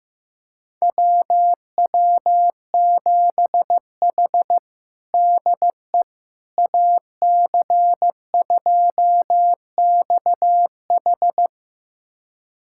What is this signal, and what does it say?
Morse code, heard as a keyed tone, about 15 wpm, 700 Hz. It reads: WW7H DE AC2XH
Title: USA Callsign DE Callsign 15wpm